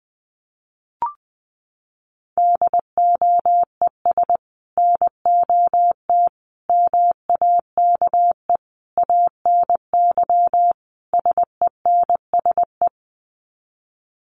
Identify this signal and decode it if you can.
Morse code, heard as a keyed tone, about 19 wpm, 700 Hz. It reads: DOES NOT MAKE ANY SENSE
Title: Sets of 5 Words Encore2 20wpm